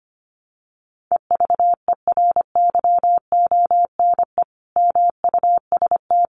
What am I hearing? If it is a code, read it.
Morse code, heard as a keyed tone, about 25 wpm, 700 Hz. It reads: EVERYONE MUST
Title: Sentences from Top 800 Words 25wpm